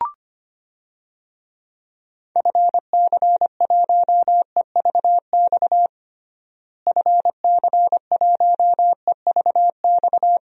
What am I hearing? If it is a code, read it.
Morse code, heard as a keyed tone, about 25 wpm, 700 Hz. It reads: FC1EVX FC1EVX
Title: DXCC-100 Call Signs - 6 Characters - Repeated 3x - 25wpm